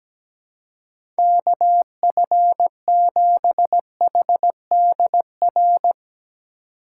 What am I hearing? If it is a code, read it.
Morse code, heard as a keyed tone, about 17 wpm, 700 Hz. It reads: KF7HDR
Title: USA Call Signs - 6 Characters - 2x Character Spacing - 17wpm